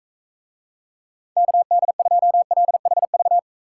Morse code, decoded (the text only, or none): KD2LHV